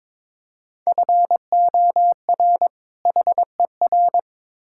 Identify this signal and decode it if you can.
Morse code, heard as a keyed tone, about 22 wpm, 700 Hz. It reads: FOR HER
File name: Build A Phrase - 2-Word Phrases - 22wpm